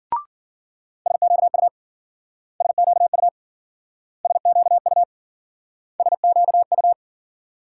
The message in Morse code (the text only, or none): SQU SQU SQU SQU